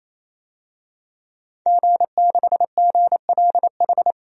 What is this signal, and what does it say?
Morse code, heard as a keyed tone, about 27 wpm, 700 Hz. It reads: G6GLH
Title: DXCC-100 Call Signs - 5 Characters - 28wpm